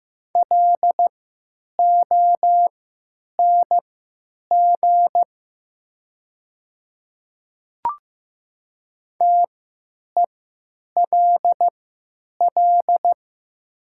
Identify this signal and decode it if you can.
Morse code, heard as a keyed tone, about 15 wpm, spaced as for 9 wpm, 700 Hz. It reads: LONG TELL
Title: 3x Character Spacing - Top 200 Words - 15wpm